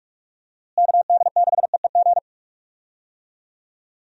Morse code, heard as a keyed tone, about 46 wpm, 700 Hz. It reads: KD6EEG
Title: USA Call Signs - 6 Characters - Repeated 2x - 45wpm